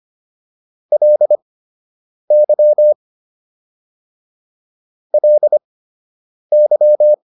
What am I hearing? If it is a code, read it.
Morse code, heard as a keyed tone, about 25 wpm, spaced as for 8 wpm, 600 Hz. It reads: LY LY